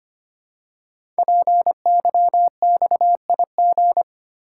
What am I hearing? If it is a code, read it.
Morse code, heard as a keyed tone, about 25 wpm, 700 Hz. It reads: PYXIG
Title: Code Groups - 5 Letters - 25wpm